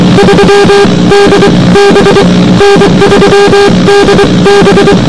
Morse code, heard as a keyed tone, about 23 wpm, spaced as for 20 wpm, 400 Hz, with noise. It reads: 3DB N3DB